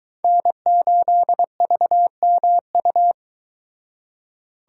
Morse code, heard as a keyed tone, about 24 wpm, 700 Hz. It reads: N8VMU